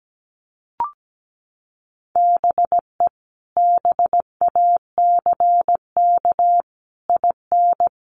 Morse code, heard as a keyed tone, about 17 wpm, 700 Hz. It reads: BE BACK IN
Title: Sets of 5 Words Encore 17wpm